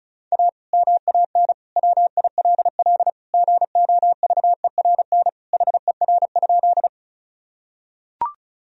Morse code, heard as a keyed tone, about 34 wpm, 700 Hz. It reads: A MAN WILL GOVERN HER?
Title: Sentences from Top 1000 Words - Review Part 2 - 35wpm